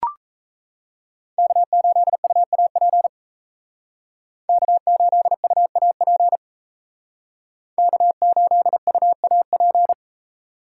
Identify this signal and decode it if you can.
Morse code, heard as a keyed tone, about 35 wpm, 700 Hz. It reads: K8UAP K8UAP K8UAP